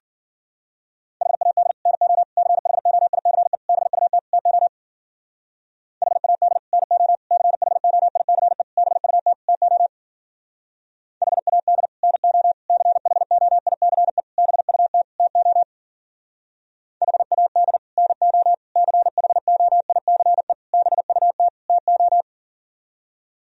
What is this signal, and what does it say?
Morse code, heard as a keyed tone, about 52 wpm, 700 Hz. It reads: HAD NO CHOICE BUT TO HAD NO CHOICE BUT TO HAD NO CHOICE BUT TO HAD NO CHOICE BUT TO